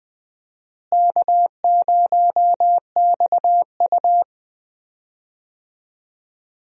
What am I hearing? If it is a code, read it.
Morse code, heard as a keyed tone, about 20 wpm, 700 Hz. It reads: K0XU
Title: USA Call Signs - 4 Characters - Repeated 2x - 20wpm